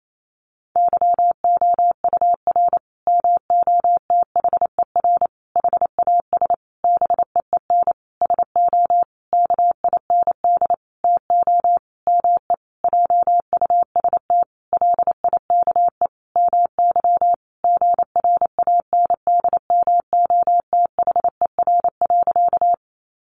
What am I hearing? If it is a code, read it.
Morse code, heard as a keyed tone, about 28 wpm, 700 Hz. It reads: YOUR MOTHER HAS BEEN SO KIND TO ME JUST LIKE MY GRANDMOTHER.
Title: Sentences from Top 1200 Words 28wpm